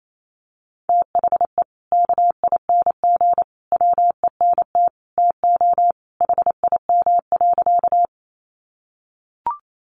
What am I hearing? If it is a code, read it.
Morse code, heard as a keyed tone, about 28 wpm, 700 Hz. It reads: THE KING WENT TO HIM.
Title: Sentences from Top 400 Words 28wpm